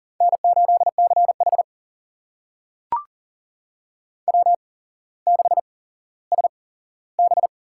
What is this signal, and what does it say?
Morse code, heard as a keyed tone, about 42 wpm, spaced as for 17 wpm, 700 Hz. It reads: N8CH W6SB
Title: USA Call Signs - 4 Characters - 8x Character Spacing - 40wpm